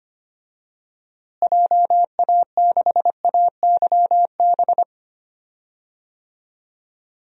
Morse code, heard as a keyed tone, about 25 wpm, 700 Hz. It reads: JA6AYB
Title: DXCC-100 Call Signs - 6 Characters - Repeated 2x - 25wpm